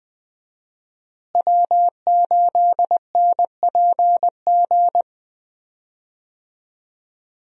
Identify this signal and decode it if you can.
Morse code, heard as a keyed tone, about 21 wpm, 700 Hz. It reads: W8NPG
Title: USA Call Signs - 5 Characters - Repeated 3x - 20wpm